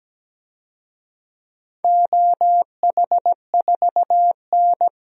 Morse code, heard as a keyed tone, about 17 wpm, 700 Hz. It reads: OH4N
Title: DXCC-100 Call Signs - 4 Characters - 17wpm